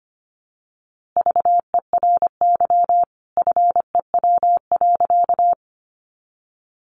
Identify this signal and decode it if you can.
Morse code, heard as a keyed tone, about 25 wpm, 700 Hz. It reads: VERY FEW.
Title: A Sound Of Thunder 25wpm